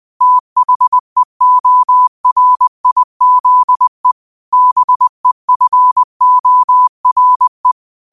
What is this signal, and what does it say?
Morse code, heard as a keyed tone, about 20 wpm, 1000 Hz. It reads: THEORIZE BEFORE